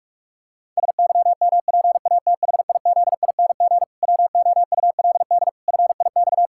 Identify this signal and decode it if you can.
Morse code, heard as a keyed tone, about 46 wpm, 700 Hz. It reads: SYMPATHIZING WOULD FIX
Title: Pangram Challenge v2 - 45wpm